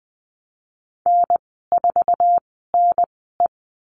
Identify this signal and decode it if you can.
Morse code, heard as a keyed tone, about 20 wpm, spaced as for 14 wpm, 700 Hz. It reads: N4NE
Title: 179 Lesson 29g - Slash - Call Signs 20wpm